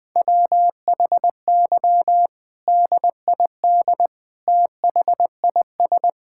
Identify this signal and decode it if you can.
Morse code, heard as a keyed tone, about 20 wpm, 700 Hz. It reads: WHY DID THIS
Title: Sentences from Top 1000 Words - Review Part 6 - 20wpm